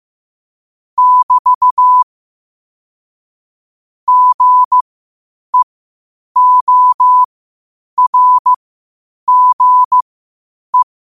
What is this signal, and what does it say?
Morse code, heard as a keyed tone, about 15 wpm, spaced as for 8 wpm, 1000 Hz. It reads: <BT> GEORGE
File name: qotdfiles / 8wpm